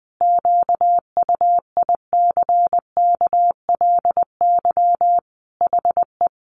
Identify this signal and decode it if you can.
Morse code, heard as a keyed tone, about 20 wpm, 700 Hz. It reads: QUICKLY HE
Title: Sentences from Top 700 Words 20wpm